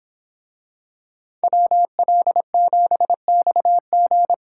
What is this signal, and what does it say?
Morse code, heard as a keyed tone, about 26 wpm, 700 Hz. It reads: WL7XG
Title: Speed Racing - USA Call Signs 22wpm